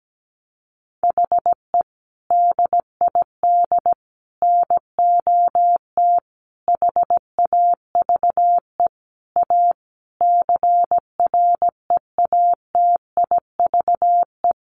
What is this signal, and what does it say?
Morse code, heard as a keyed tone, about 17 wpm, 700 Hz. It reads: HE DID NOT HAVE A CREATIVE
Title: Sentences from Top 1300 Words 17wpm